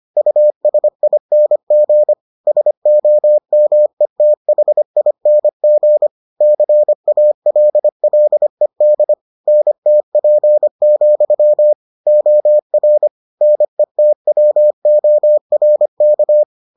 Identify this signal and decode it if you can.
Morse code, heard as a keyed tone, about 25 wpm, 600 Hz. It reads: USING SOMETHING CALLED NTP, OR NETWORK